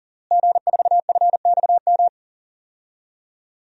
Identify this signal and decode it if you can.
Morse code, heard as a keyed tone, about 40 wpm, 700 Hz. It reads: G4FXM
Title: DXCC-100 Call Signs - 5 Characters - Repeated 2x - 40wpm